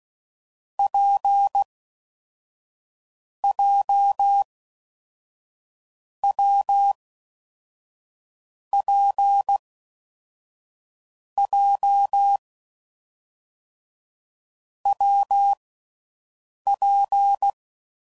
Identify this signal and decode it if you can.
Morse code, heard as a keyed tone, about 16 wpm, 800 Hz. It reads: P J W P J W P
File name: Combinations / Difficult Combos / WPJ WPJ1